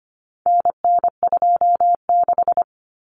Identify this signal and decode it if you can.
Morse code, heard as a keyed tone, about 25 wpm, 700 Hz. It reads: NN26
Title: Common Cut Numbers - 4-Digit 25wpm